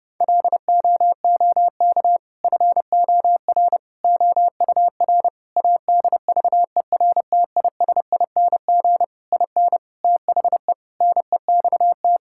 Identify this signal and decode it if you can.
Morse code, heard as a keyed tone, about 30 wpm, 700 Hz. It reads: LOOK FOR OUR ADVERTISING IN THE NEXT
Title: Sentences from Top 1000 Words - Review Part 6 - 30wpm